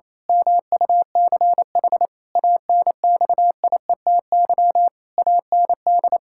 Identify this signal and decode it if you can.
Morse code, heard as a keyed tone, about 28 wpm, 700 Hz. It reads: MUCH ANXIETY AND
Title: Pangram Challenge v2 - 28wpm